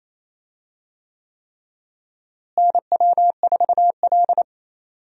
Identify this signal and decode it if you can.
Morse code, heard as a keyed tone, about 28 wpm, 700 Hz. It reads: NW4L
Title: USA Call Signs - 4 Characters - Repeated 3x - 28wpm